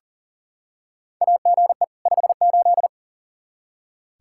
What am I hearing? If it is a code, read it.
Morse code, heard as a keyed tone, about 40 wpm, 700 Hz. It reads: AGE 58
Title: Practice QSO - Exchange 3 - More Info - 40wpm